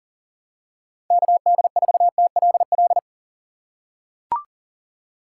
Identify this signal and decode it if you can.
Morse code, heard as a keyed tone, about 40 wpm, 700 Hz. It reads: KD4TLL
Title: USA Call Signs - 4x Character Spacing - 40wpm